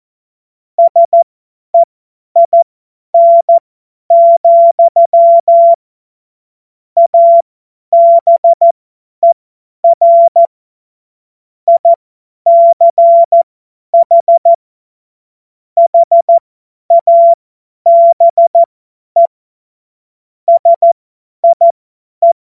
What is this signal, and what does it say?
Morse code, heard as a keyed tone, about 14 wpm, spaced as for 10 wpm, 680 Hz. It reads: SEIN, ABER ICH HABE SIE
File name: Morsekurs / Witze / Witz 27. Dezember 2022 50 F70